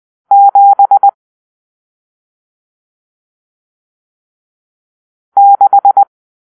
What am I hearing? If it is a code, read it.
Morse code, heard as a keyed tone, about 20 wpm, 800 Hz. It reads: 7 6
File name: arrl / Grp-7 Drill-1